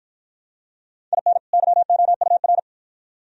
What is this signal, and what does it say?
Morse code, heard as a keyed tone, about 53 wpm, 700 Hz. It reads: IN YOUR